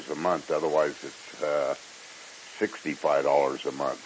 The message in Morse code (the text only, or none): none